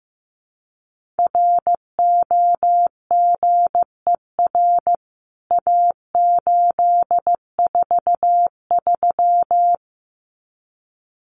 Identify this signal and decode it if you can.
Morse code, heard as a keyed tone, about 15 wpm, 700 Hz. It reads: ROGER A843
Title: CWT Exchange 15wpm